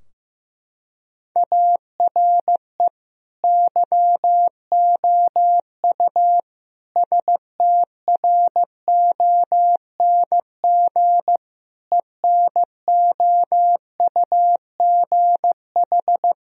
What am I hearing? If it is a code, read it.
Morse code, heard as a keyed tone, about 15 wpm, 700 Hz. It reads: ARE YOU STRONG ENOUGH
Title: Sentences from Top 1000 Words - Review Part 3 - 15wpm